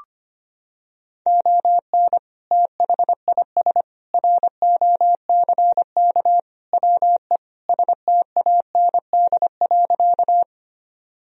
Morse code, heard as a keyed tone, about 25 wpm, 700 Hz. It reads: ON THIS ROCK WE STAND.
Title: Sentences from Top 1000 Words - Review Part 2 - 25wpm